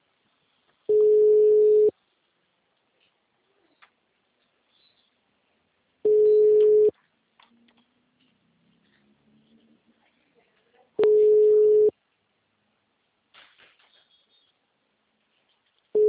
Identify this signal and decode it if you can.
Ringback tone (Europe, 425 Hz), with speech.
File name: out- -2026- - - .78522